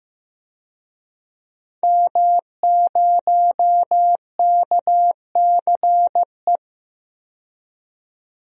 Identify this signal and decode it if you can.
Morse code, heard as a keyed tone, about 15 wpm, 700 Hz. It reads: M0KCE
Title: DXCC-100 Call Signs - 5 Characters - 15wpm